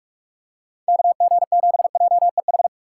Morse code, heard as a keyed tone, about 44 wpm, 700 Hz. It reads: KG7JEH
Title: USA Call Signs - 6 Characters - 6x Character Spacing - 45wpm